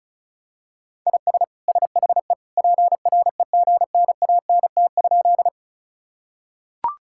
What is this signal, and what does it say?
Morse code, heard as a keyed tone, about 36 wpm, 700 Hz. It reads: IS SHE PREGNANT?
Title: Sentences from Top 1000 Words - Review Part 6 - 35wpm